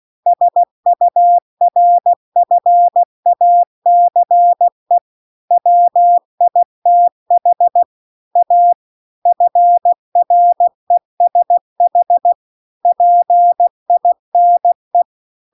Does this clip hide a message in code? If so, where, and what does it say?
Yes: Morse code from 0.3 to 15.0 s, heard as a keyed tone, about 16 wpm, 700 Hz. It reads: SURFACE WITH A FRESH PINE